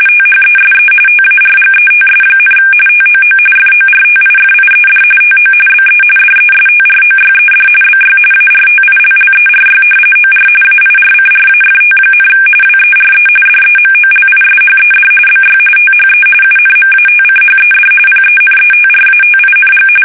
RTTY, about 45.45 baud. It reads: THESE TARGETS, TOO FORTIFIED TO BE DESTROYED BY ARTILLERY OR AIRCRAFT, WERE DEMOLISHED BY THE UNMATCHED PENETRATING
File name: fsk850 / battleshipnj